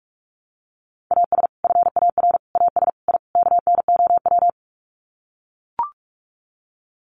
Morse code, heard as a keyed tone, about 44 wpm, 700 Hz. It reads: AS FAR AS I KNOW
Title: Sets of 5 Words 45wpm